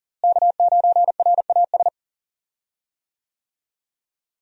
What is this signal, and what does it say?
Morse code, heard as a keyed tone, about 40 wpm, 700 Hz. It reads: K9RAS
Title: USA Call Signs - 5 Characters - Repeated 2x - 40wpm